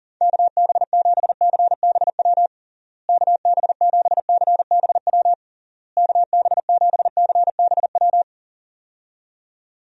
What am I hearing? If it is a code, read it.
Morse code, heard as a keyed tone, about 40 wpm, 700 Hz. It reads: KB7CBW KB7CBW KB7CBW
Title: USA Call Signs - 6 Characters - Repeated 3x - 40wpm